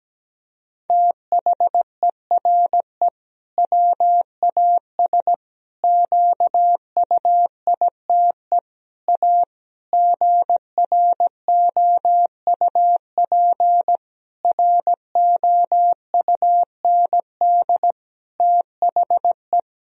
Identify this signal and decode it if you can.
Morse code, heard as a keyed tone, about 17 wpm, 700 Hz. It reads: THERE WAS QUITE A GROUP ROUND THE
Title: Sentences from Top 1000 Words - Review Part 1 - 17wpm